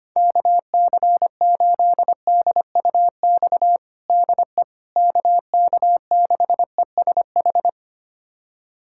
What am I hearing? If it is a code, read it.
Morse code, heard as a keyed tone, about 25 wpm, 700 Hz. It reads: KC8DUX DE KK6ESH
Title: USA Callsign DE Callsign 25wpm